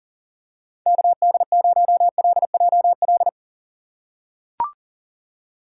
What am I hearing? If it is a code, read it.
Morse code, heard as a keyed tone, about 40 wpm, 700 Hz. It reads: KD0LJL
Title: USA Call Signs - 2x Character Spacing - 40wpm